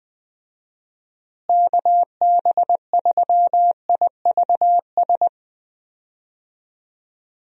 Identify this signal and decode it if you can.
Morse code, heard as a keyed tone, about 21 wpm, 700 Hz. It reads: KB3IVS
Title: USA Call Signs - 6 Characters - Repeated 2x - 20wpm